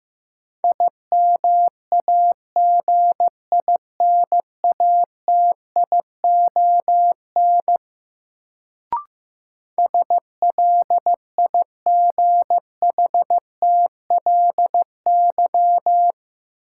Morse code, heard as a keyed tone, about 15 wpm, 700 Hz. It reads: IMAGINATION SLIGHTLY
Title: Sentences from Top 1300 Words - New Word Drill Alt - 15wpm